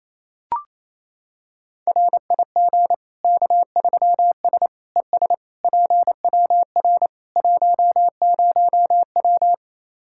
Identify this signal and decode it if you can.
Morse code, heard as a keyed tone, about 28 wpm, 700 Hz. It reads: RIG K3S ES PWR 10W
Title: Practice QSO - Exchange 2 - Info - 28wpm